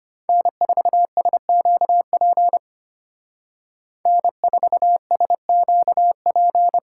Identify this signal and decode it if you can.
Morse code, heard as a keyed tone, about 30 wpm, 700 Hz. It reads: N4SQP N4SQP
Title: Speed Racing - USA 1x3 Call Signs 25wpm